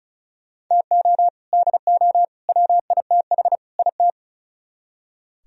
Morse code, heard as a keyed tone, about 43 wpm, 700 Hz. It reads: TO DO WITH IT
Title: Speed-Racing - Sets of 4 Words 35wpm